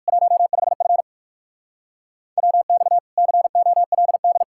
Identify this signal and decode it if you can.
Morse code, heard as a keyed tone, about 59 wpm, 700 Hz. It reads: 15F WX COLD